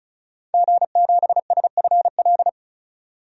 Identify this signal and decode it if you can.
Morse code, heard as a keyed tone, about 34 wpm, 700 Hz. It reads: G7SFL
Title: DXCC-100 Call Signs - 5 Characters - 35wpm